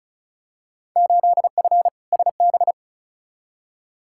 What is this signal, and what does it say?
Morse code, heard as a keyed tone, about 36 wpm, 690 Hz. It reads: 8F SB
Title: Field Day Exchanges Encore - 35wpm